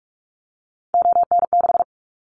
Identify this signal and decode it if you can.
Morse code, heard as a keyed tone, about 46 wpm, 700 Hz. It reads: ON6
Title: Callsign Prefixes for Top 25 DXCC - 45wpm